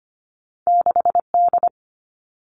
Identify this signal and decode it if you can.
Morse code, heard as a keyed tone, about 25 wpm, 700 Hz. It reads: 6D
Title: All Cut Numbers - 2-Digit 25wpm